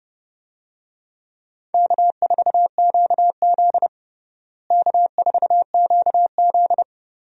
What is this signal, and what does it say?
Morse code, heard as a keyed tone, about 30 wpm, 700 Hz. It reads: K4QZ K4QZ
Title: USA Call Signs - 4 Characters - Repeated 2x - 30wpm